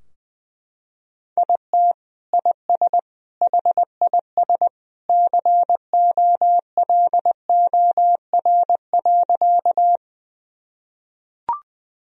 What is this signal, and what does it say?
Morse code, heard as a keyed tone, about 20 wpm, 700 Hz. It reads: IT IS HIS COLOR.
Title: Sentences from Top 1000 Words - Review Part 2 - 20wpm